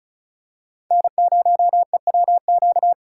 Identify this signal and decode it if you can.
Morse code, heard as a keyed tone, about 34 wpm, 700 Hz. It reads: N0EWQ
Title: USA Call Signs - 2x Character Spacing - 35wpm